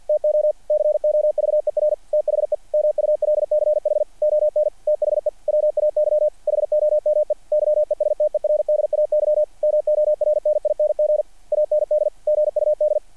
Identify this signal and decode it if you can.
Morse code, heard as a keyed tone, about 48 wpm, 660 Hz. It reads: TO COVER THE MULCH ON THE WAY HOME YESTERDAY MORNING AND GUD
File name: archived / CRSnet